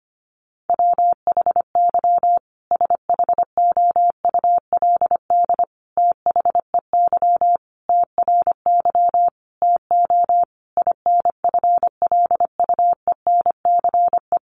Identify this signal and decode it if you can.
Morse code, heard as a keyed tone, about 25 wpm, 700 Hz. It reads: WHY SHOULD THEY TRY TO INFLUENCE
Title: Sentences from Top 1100 Words 25wpm